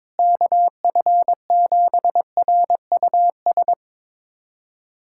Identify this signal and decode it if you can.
Morse code, heard as a keyed tone, about 22 wpm, 700 Hz. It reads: KF7RUS
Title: USA Callsign DE Callsign - 6 Characters - 22wpm